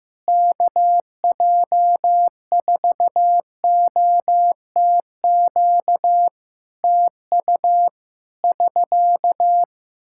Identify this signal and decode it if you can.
Morse code, heard as a keyed tone, about 16 wpm, 700 Hz. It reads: KJ4OTQ TU <SK>
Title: Practice QSO - Exchange 4 - Ending - 15wpm